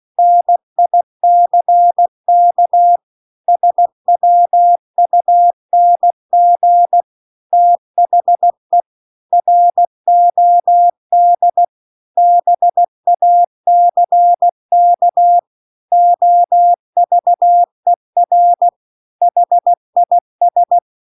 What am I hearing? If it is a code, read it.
Morse code, heard as a keyed tone, about 16 wpm, 700 Hz. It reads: NICK SWUNG THE ROD BACK OVER HIS